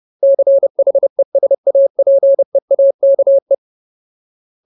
Morse code, heard as a keyed tone, about 30 wpm, 550 Hz. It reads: CHESAPEAKE
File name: Ultimate Word List / 30WPM Ultimate